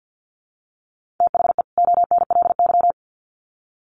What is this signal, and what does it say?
Morse code, heard as a keyed tone, about 50 wpm, 700 Hz. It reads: THE ONLY
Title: Sets of 2 Words 50wpm